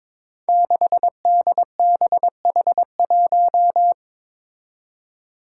Morse code, heard as a keyed tone, about 23 wpm, 700 Hz. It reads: 6DBH1
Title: Five Challenging Letter-Number Combinations 22wpm